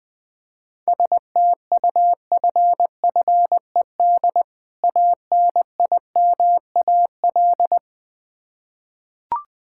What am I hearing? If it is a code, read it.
Morse code, heard as a keyed tone, about 20 wpm, 690 Hz. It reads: STUFFED ANIMAL
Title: Difficult Words with Hints - 20wpm